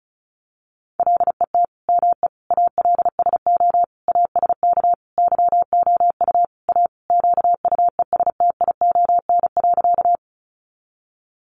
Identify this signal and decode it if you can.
Morse code, heard as a keyed tone, about 34 wpm, 700 Hz. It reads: LET ME ALSO ASK YOU A QUESTION.
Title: Sentences from Top 400 Words 35wpm